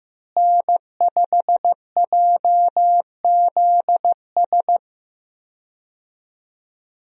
Morse code, heard as a keyed tone, about 15 wpm, 700 Hz. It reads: N5JZS
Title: USA Call Signs - 5 Characters - Repeated 2x - 15wpm